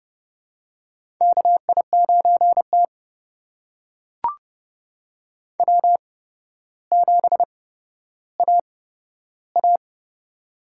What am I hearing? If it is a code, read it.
Morse code, heard as a keyed tone, about 30 wpm, spaced as for 13 wpm, 700 Hz. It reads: KI9T W7AA
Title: USA Call Signs - 4 Characters - 8x Character Spacing - 30wpm